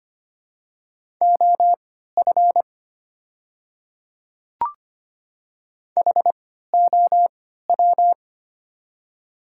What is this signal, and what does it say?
Morse code, heard as a keyed tone, about 25 wpm, spaced as for 14 wpm, 690 Hz. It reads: OF HOW
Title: 3x Character Spacing - Top 100 Words - 25wpm